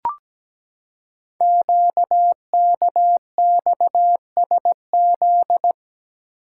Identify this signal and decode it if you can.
Morse code, heard as a keyed tone, about 17 wpm, 700 Hz. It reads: QKXSZ
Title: Code Groups - 5 Letters - 17wpm